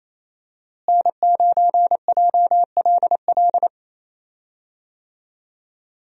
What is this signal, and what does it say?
Morse code, heard as a keyed tone, about 27 wpm, 700 Hz. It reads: N9JLL
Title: USA Call Signs - 5 Characters - Repeated 3x - 28wpm